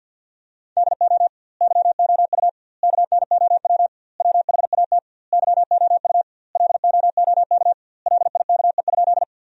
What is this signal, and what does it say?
Morse code, heard as a keyed tone, about 50 wpm, 700 Hz. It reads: DO YOU KNOW WHAT YOU LOOK LIKE?
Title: 2x Word-Spacing - Sentences from Top 100 Words - 50wpm